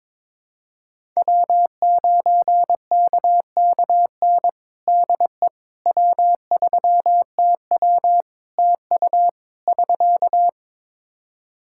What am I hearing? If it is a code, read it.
Morse code, heard as a keyed tone, about 22 wpm, 700 Hz. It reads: W9KKN DE W3TW TU <SK>
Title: Practice QSO - Exchange 4 - Ending - 22wpm